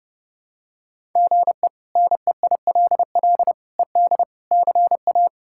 Morse code, heard as a keyed tone, about 32 wpm, 700 Hz. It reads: GE NEILL ED CA
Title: K1USN SST QSO Practice - 30wpm